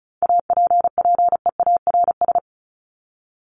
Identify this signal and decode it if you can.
Morse code, heard as a keyed tone, about 34 wpm, 700 Hz. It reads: APPEARS
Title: ES:S Suffix - Top 100 Words with Base Words 35wpm